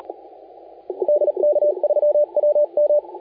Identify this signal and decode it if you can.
Morse code, heard as a keyed tone, about 38 wpm, 590 Hz, with radio noise. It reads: DK3WM
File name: la4c / sac-cw-2018 / CONTEST AUDIO 1600Z